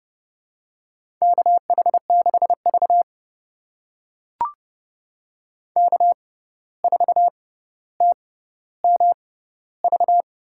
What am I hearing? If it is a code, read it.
Morse code, heard as a keyed tone, about 31 wpm, spaced as for 14 wpm, 700 Hz. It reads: KH6V K4TMV
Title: USA Call Signs - 6x Character Spacing - 30wpm